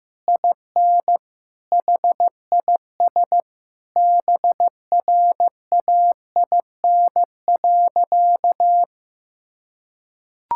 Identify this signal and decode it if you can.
Morse code, heard as a keyed tone, about 15 wpm, 700 Hz. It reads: IN HIS BRAIN.
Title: Sentences from Top 1600 Words 15wpm